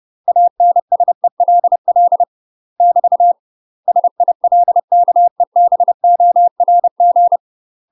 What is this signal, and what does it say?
Morse code, heard as a keyed tone, about 30 wpm, 700 Hz. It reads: ANSELL <BT> SILKEBORG
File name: UK MASTER QSOs TRAINING / 30WPM / UK MASTER QSOs 0040